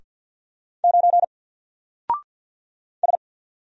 Morse code, heard as a keyed tone, about 48 wpm, 700 Hz. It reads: 9 S
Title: Single Character (Letter:Number:Punctuation:Prosign) 50wpm